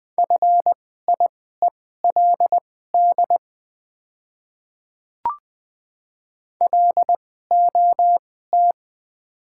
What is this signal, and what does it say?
Morse code, heard as a keyed tone, about 20 wpm, spaced as for 15 wpm, 690 Hz. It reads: FIELD LOT
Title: 2x Character Spacing - Top 500 Words - 20wpm